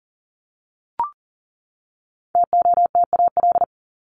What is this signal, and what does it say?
Morse code, heard as a keyed tone, about 40 wpm, 700 Hz. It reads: TOTAL
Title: Top 1251-1500 Words 40wpm